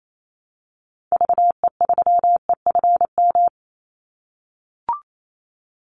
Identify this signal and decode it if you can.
Morse code, heard as a keyed tone, about 28 wpm, 700 Hz. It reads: VE3EFM
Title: Call Signs - 28wpm